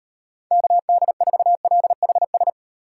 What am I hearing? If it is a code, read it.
Morse code, heard as a keyed tone, about 38 wpm, 700 Hz. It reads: KD4LHS